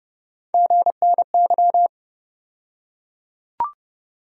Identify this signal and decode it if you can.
Morse code, heard as a keyed tone, about 31 wpm, 700 Hz. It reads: GNY
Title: USA Call Sign Postfixes - 3 Letters 30wpm